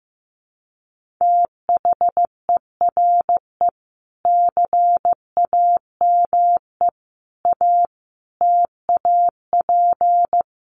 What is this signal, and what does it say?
Morse code, heard as a keyed tone, about 14 wpm, 700 Hz. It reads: THERE CAME A TAP
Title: Sentences from Top 1400 Words 15wpm